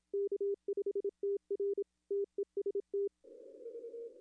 Morse code, heard as a keyed tone, about 27 wpm, 400 Hz. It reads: K5TR TEST
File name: contests / arrl-160-k5tr-op-w5tn -0800